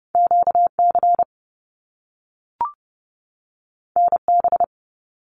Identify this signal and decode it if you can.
Morse code, heard as a keyed tone, about 30 wpm, 700 Hz. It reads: QC NB
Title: Canadian Provinces and Territories 30wpm